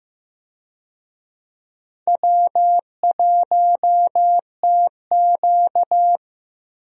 Morse code, heard as a keyed tone, about 15 wpm, 700 Hz. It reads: W1TQ
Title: USA Call Signs - 4 Characters - Repeated 2x - 15wpm